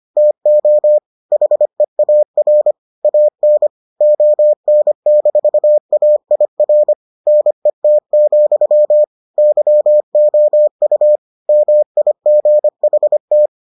Morse code, heard as a keyed tone, about 25 wpm, 600 Hz. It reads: TO HEAR AN ON-AIR NET, YOU MIGHT